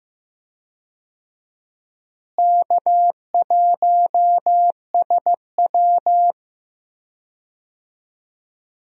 Morse code, heard as a keyed tone, about 15 wpm, 700 Hz. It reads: K1SW
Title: USA Call Signs - 4 Characters - Repeated 3x - 15wpm